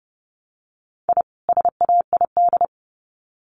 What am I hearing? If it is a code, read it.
Morse code, heard as a keyed tone, about 30 wpm, 700 Hz. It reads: I SAID
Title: Sets of 2 Words 30wpm